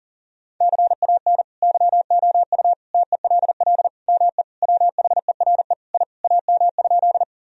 Morse code, heard as a keyed tone, about 40 wpm, 700 Hz. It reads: CAN YOU TELL ME WHERE I AM?
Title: Sentences from Top 1000 Words - Review Part 2 - 40wpm